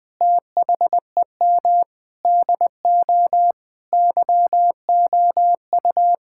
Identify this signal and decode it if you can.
Morse code, heard as a keyed tone, about 20 wpm, 700 Hz. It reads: THEM DO YOU
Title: Sentences from Top 1000 Words - Review Part 5 - 20wpm